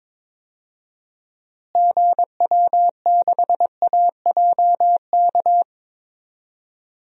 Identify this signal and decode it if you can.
Morse code, heard as a keyed tone, about 22 wpm, 700 Hz. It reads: GW6AJK
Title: DXCC-100 Call Signs - 6 Characters - Repeated 2x - 22wpm